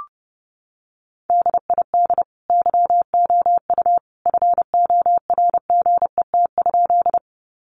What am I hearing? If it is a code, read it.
Morse code, heard as a keyed tone, about 30 wpm, 700 Hz. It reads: DID YOU FORGET?
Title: Sentences from Top 1200 Words 30wpm